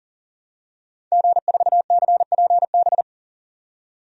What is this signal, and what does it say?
Morse code, heard as a keyed tone, about 40 wpm, 700 Hz. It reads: G4CPB
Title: DXCC-100 Call Signs - 5 Characters - 40wpm